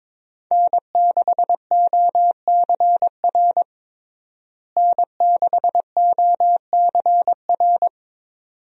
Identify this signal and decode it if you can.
Morse code, heard as a keyed tone, about 21 wpm, 700 Hz. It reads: N6OCR N6OCR
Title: USA Call Signs - 5 Characters - Repeated 2x - 22wpm